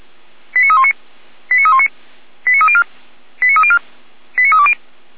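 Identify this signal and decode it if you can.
Five-tone selective call (ZVEI), heard as quick runs of five tones, tones of about 70 ms. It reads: 89319 89319 89304 89304 89320
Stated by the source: ZVEI five-tone call received by SDR, from feuerwehr-orlamuende.org (fire brigade Orlamünde).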